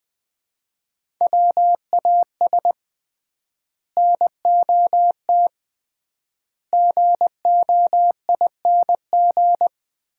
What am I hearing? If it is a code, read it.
Morse code, heard as a keyed tone, about 20 wpm, 700 Hz. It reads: WAS NOT GOING
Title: 3x Word-Spacing - Sets of 4 Words - 20wpm